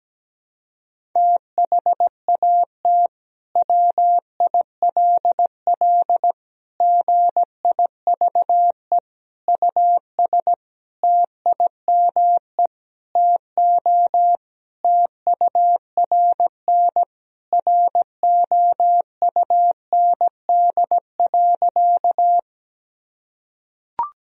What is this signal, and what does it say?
Morse code, heard as a keyed tone, about 17 wpm, 700 Hz. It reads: THAT WILL GIVE US TIME TO TURN ROUND.
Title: Sentences from Top 1000 Words - Review Part 1 - 17wpm